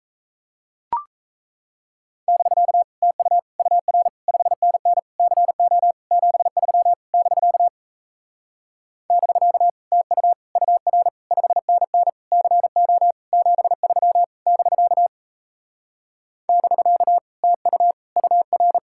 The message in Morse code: <BK> TU UR 5NN CO 73 <BK> <BK> TU UR 5NN CO 73 <BK> <BK> TU UR

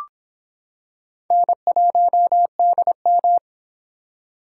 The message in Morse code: N1DM

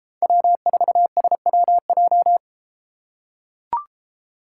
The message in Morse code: W4SWJ

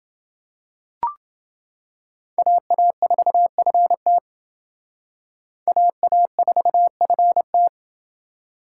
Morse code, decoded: AA4FT AA4FT

